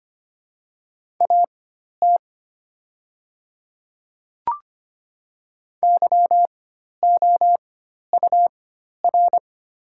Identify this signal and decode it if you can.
Morse code, heard as a keyed tone, about 25 wpm, spaced as for 12 wpm, 690 Hz. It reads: AT YOUR
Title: 4x Character Spacing - Top 200 Words - 25wpm